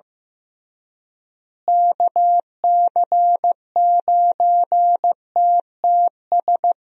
Morse code, heard as a keyed tone, about 15 wpm, 700 Hz. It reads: KC9TTS
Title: USA Call Signs - 6 Characters - Repeated 3x - 15wpm